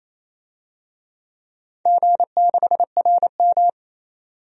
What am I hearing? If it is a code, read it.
Morse code, heard as a keyed tone, about 28 wpm, 700 Hz. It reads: G6RM
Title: DXCC-100 Call Signs - 4 Characters - 28wpm